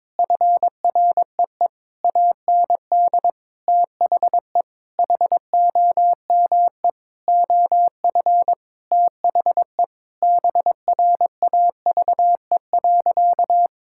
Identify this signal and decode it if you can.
Morse code, heard as a keyed tone, about 22 wpm, 700 Hz. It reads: FREE AND THE HOME OF THE BRAVE.
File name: practice / 22wpm / Sentences with 8-Word Phrases - 22wpm